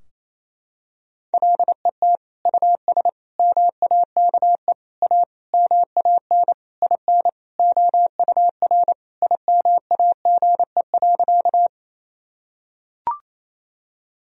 Morse code, heard as a keyed tone, about 28 wpm, 700 Hz. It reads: LET US MAKE A MAN IN OUR IMAGE.
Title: Sentences from Top 1000 Words - Review Part 4 - 28wpm